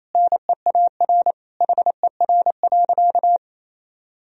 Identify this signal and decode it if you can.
Morse code, heard as a keyed tone, about 28 wpm, 700 Hz. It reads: NEAR HER.
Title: Sentences from Top 1000 Words - Review Part 1 - 28wpm